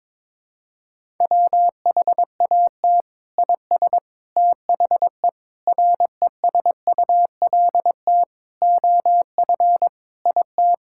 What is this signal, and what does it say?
Morse code, heard as a keyed tone, about 22 wpm, 700 Hz. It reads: WHAT IS THE RESULT OF IT
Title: Sentences from Top 1000 Words - Review Part 5 - 22wpm